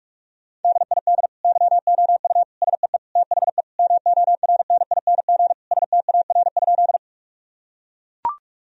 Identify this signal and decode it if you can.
Morse code, heard as a keyed tone, about 43 wpm, 700 Hz. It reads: DID YOU SEE THE MORNING STAR?
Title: Sentences from Top 1000 Words - Review Part 2 - 45wpm